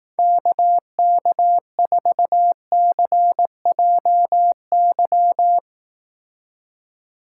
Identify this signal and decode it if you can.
Morse code, heard as a keyed tone, about 18 wpm, 700 Hz. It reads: KK4CJY
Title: Speed Racing - USA Call Signs 15wpm